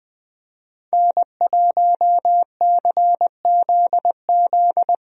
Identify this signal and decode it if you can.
Morse code, heard as a keyed tone, about 20 wpm, 700 Hz. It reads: N1CZZ
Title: USA Call Signs - 5 Characters - Repeated 3x - 20wpm